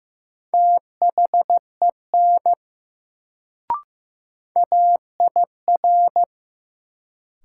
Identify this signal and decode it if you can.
Morse code, heard as a keyed tone, about 15 wpm, 700 Hz. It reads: THEN AIR
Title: Sets of 2 Words - New Word Drill - 15wpm